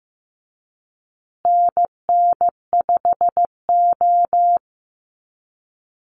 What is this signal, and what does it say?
Morse code, heard as a keyed tone, about 15 wpm, 700 Hz. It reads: NN5O
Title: Call Signs Repeated 2x - 15wpm